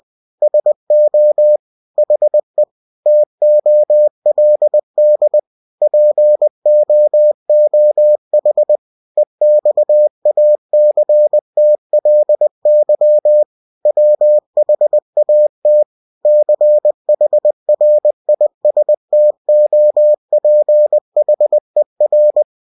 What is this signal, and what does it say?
Morse code, heard as a keyed tone, about 20 wpm, 600 Hz. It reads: SO HE TOLD POOH EXACTLY WHAT CHRISTOPHER
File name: cw-books / winnie-the-pooh / winnie-20wpm-page-0103